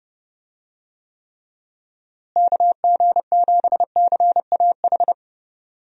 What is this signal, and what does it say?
Morse code, heard as a keyed tone, about 30 wpm, 700 Hz. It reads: KG7CAH
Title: USA Call Signs - 6 Characters - Repeated 2x - 30wpm